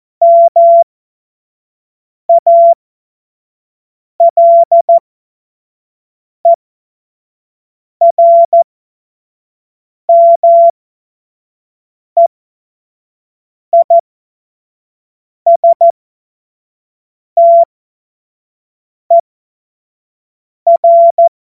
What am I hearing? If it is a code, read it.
Morse code, heard as a keyed tone, about 14 wpm, spaced as for 5 wpm, 680 Hz. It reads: MALERMEISTER
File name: Morsekurs / Witze / Witz 12 Dezember 2023 25 F70